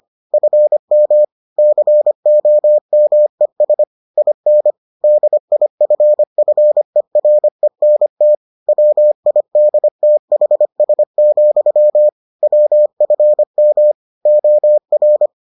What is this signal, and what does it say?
Morse code, heard as a keyed tone, about 25 wpm, 600 Hz. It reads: FM COMES IN DIFFERENT WIDTHS, WFM OR